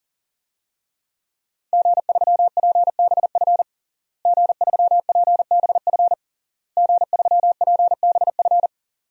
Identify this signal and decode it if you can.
Morse code, heard as a keyed tone, about 38 wpm, 700 Hz. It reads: G3PBF G3PBF G3PBF
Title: DXCC-100 Call Signs - 5 Characters - Repeated 3x - 40wpm